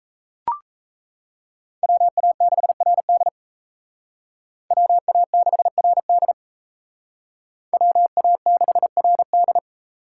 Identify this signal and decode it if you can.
Morse code, heard as a keyed tone, about 36 wpm, 700 Hz. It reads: WA6RD WA6RD WA6RD